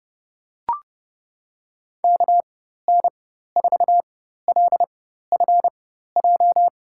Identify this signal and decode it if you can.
Morse code, heard as a keyed tone, about 30 wpm, spaced as for 14 wpm, 700 Hz. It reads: KN4LFJ
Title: USA Call Signs - 6 Characters - 4x Character Spacing - 30wpm